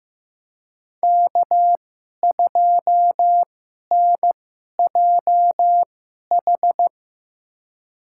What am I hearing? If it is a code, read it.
Morse code, heard as a keyed tone, about 15 wpm, spaced as for 11 wpm, 700 Hz. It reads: K2NJH
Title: USA Call Signs - 2x Character Spacing - 15wpm